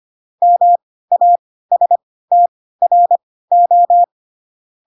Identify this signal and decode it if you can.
Morse code, heard as a keyed tone, about 25 wpm, spaced as for 16 wpm, 700 Hz. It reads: MASTRO
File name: Pinocchio / Pinocchio ch1 10 25 ik1hge